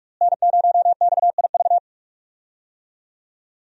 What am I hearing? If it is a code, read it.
Morse code, heard as a keyed tone, about 44 wpm, 700 Hz. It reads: N0XIV
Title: USA Call Signs - 5 Characters - Repeated 3x - 45wpm